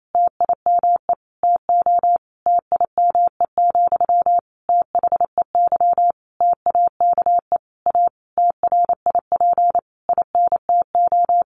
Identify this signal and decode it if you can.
Morse code, heard as a keyed tone, about 28 wpm, 700 Hz. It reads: TIME TO TIME, THEY TAKE A TRIP INTO
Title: Sentences from Top 1300 Words 28wpm